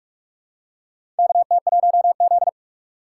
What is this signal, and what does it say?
Morse code, heard as a keyed tone, about 45 wpm, 700 Hz. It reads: KT1Z
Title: USA Call Signs - 4 Characters - 2x Character Spacing - 45wpm